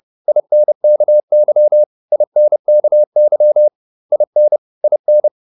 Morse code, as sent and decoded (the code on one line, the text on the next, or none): .. -. -.- -.-- / .. -. -.- -.-- / .. -. / .. -.
INKY INKY IN IN